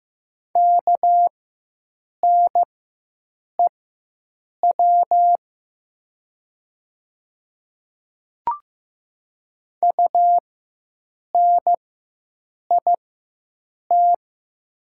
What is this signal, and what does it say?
Morse code, heard as a keyed tone, about 15 wpm, spaced as for 7 wpm, 700 Hz. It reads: KNEW UNIT
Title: 4x Character Spacing - Top 500 Words - 15wpm